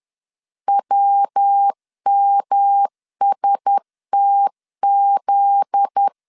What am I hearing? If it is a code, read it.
Morse code, heard as a keyed tone, about 10 wpm, 800 Hz. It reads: WMSTZ
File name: LEZIONI / 10 SIGNS / 10B INT 10wpm 2844